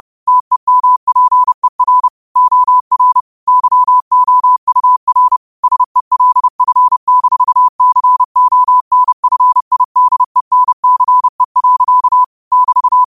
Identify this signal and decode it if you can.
Morse code, heard as a keyed tone, about 30 wpm, 1000 Hz. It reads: TEMPER OR YOUR SELF-CONFIDENCE. <BT>
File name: qotdfiles / 30wpm